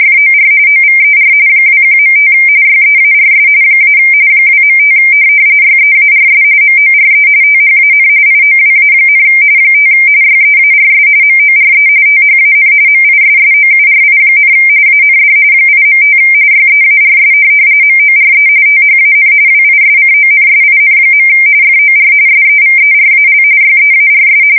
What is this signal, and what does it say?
RTTY, about 45.45 baud. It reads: FOR OVER 21.5 YEARS - 2.5 YEARS MORE THAN IOWA, 5 YEARS MORE THAN MISSOURI, AND 8.5 YEARS MORE THAN WISCONSIN. FOR SEVERAL YEARS
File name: fsk170 / USS-New-Jersey